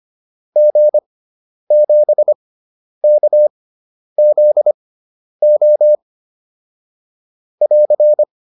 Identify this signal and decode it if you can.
Morse code, heard as a keyed tone, about 25 wpm, spaced as for 10 wpm, 600 Hz. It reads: G7KZO <AR>
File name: QSO404 10